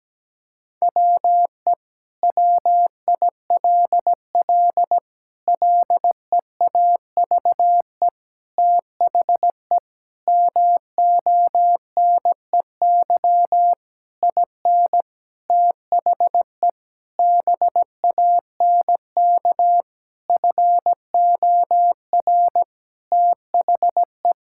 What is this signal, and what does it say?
Morse code, heard as a keyed tone, about 17 wpm, 700 Hz. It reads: WE WILL LEAVE THE MONEY IN THE BANK FOR THE
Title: Sentences from Top 1000 Words - Review Part 6 - 17wpm